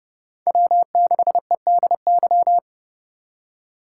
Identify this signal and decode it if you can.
Morse code, heard as a keyed tone, about 30 wpm, 700 Hz. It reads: W6EDY
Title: K1USN SST QSO Practice - 30wpm